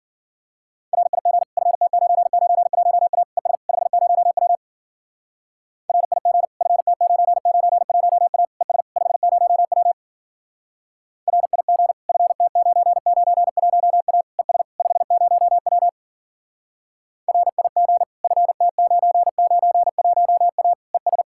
Morse code, heard as a keyed tone, about 51 wpm, 700 Hz. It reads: RIG FT991A ES 50W RIG FT991A ES 50W RIG FT991A ES 50W RIG FT991A ES